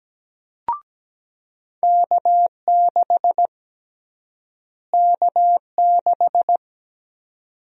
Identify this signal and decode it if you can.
Morse code, heard as a keyed tone, about 17 wpm, 700 Hz. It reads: K6 K6
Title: USA Call Sign Prefixes - Repeated 2x - 17wpm